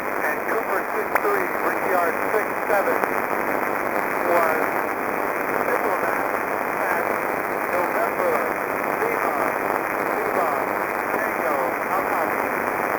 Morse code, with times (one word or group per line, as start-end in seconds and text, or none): none